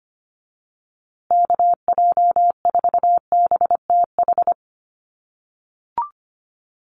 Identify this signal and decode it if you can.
Morse code, heard as a keyed tone, about 25 wpm, 700 Hz. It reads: KJ4BTH
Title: USA Call Signs Repeated 3x 25wpm